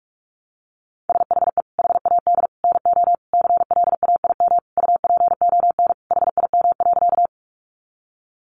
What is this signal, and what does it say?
Morse code, heard as a keyed tone, about 44 wpm, 700 Hz. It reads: SHE HAD NO CLAIM UPON HIM.
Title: Sentences from Top 1500 Words 45wpm